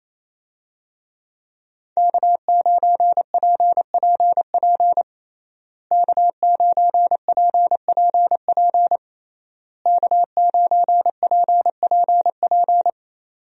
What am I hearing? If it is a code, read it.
Morse code, heard as a keyed tone, about 28 wpm, 700 Hz. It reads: K9PPP K9PPP K9PPP
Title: USA Call Signs - 5 Characters - Repeated 3x - 28wpm